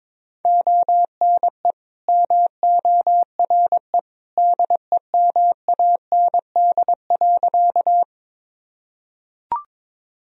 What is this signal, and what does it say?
Morse code, heard as a keyed tone, about 22 wpm, 700 Hz. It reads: ONE MORE DEMAND.
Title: Sentences from Top 1000 Words - Review Part 4 - 22wpm